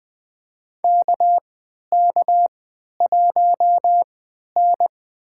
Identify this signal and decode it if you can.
Morse code, heard as a keyed tone, about 20 wpm, spaced as for 11 wpm, 700 Hz. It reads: KK1N
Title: USA Call Signs - 3x Character Spacing - 20wpm